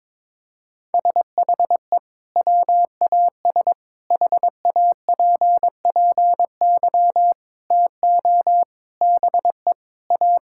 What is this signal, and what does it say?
Morse code, heard as a keyed tone, about 22 wpm, 700 Hz. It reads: SHE WAS HAPPY TO BE A
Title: Sentences from Top 1000 Words - Review Part 5 - 22wpm